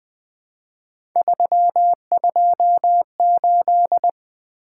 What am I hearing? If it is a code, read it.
Morse code, heard as a keyed tone, about 20 wpm, 700 Hz. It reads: 328
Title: Three Letter-Number Combinations - 20wpm